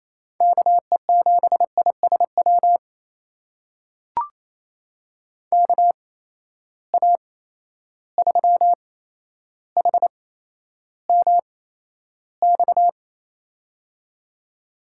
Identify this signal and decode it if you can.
Morse code, heard as a keyed tone, about 28 wpm, spaced as for 12 wpm, 700 Hz. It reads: KE7ISW KA3HMX
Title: USA Call Signs - 6 Characters - 8x Character Spacing - 28wpm